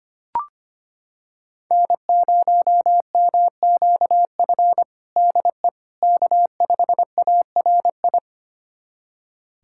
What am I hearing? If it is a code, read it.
Morse code, heard as a keyed tone, about 25 wpm, 700 Hz. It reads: N0MQF DE K5ARI
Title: USA Callsign DE Callsign 25wpm